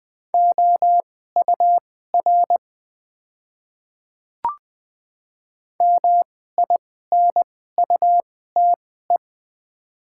Morse code, heard as a keyed tone, about 21 wpm, spaced as for 15 wpm, 700 Hz. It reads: OUR MINUTE
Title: 2x Character Spacing - Top 500 Words - 20wpm